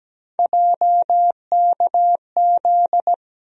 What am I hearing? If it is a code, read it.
Morse code, heard as a keyed tone, about 17 wpm, 700 Hz. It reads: JKZ
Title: USA Call Sign Postfixes - 3 Letters 17wpm